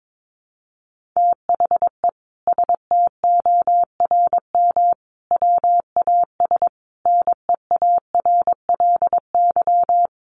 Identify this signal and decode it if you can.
Morse code, heard as a keyed tone, about 22 wpm, 700 Hz. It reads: THE STORM WAS NEARLY
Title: Sentences from Top 1500 Words 22wpm